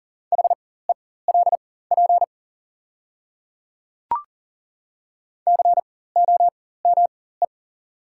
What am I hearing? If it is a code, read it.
Morse code, heard as a keyed tone, about 40 wpm, spaced as for 19 wpm, 690 Hz. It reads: HELP COME
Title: 4x Character Spacing - Top 200 Words - 40wpm